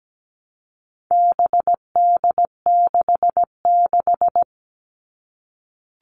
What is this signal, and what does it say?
Morse code, heard as a keyed tone, about 17 wpm, 700 Hz. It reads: BD66
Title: All Cut Numbers - 4-Digit 17wpm